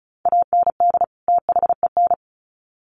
Morse code, heard as a keyed tone, about 34 wpm, 700 Hz. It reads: AND THEN
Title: Sets of 2 Words 35wpm